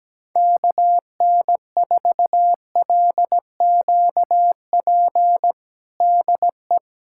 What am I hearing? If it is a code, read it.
Morse code, heard as a keyed tone, about 17 wpm, 700 Hz. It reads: KN4LQP DE
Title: USA Callsign DE Callsign 17wpm